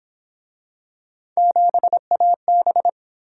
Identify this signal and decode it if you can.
Morse code, heard as a keyed tone, about 29 wpm, 700 Hz. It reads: 7AB